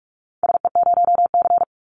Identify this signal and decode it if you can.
Morse code, heard as a keyed tone, about 46 wpm, 700 Hz. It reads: SE0C
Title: Call Signs - 45wpm